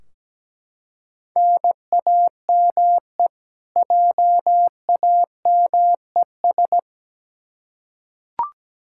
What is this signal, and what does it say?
Morse code, heard as a keyed tone, about 17 wpm, 700 Hz. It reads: NAME JAMES
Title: Practice QSO - Exchange 1 - Beginning - 17wpm